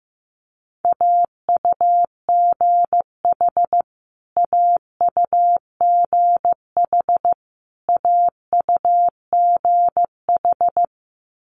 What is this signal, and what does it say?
Morse code, heard as a keyed tone, about 15 wpm, 700 Hz. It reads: AUGH AUGH AUGH
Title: Multi-letter Phonemes Repeated 3x 15wpm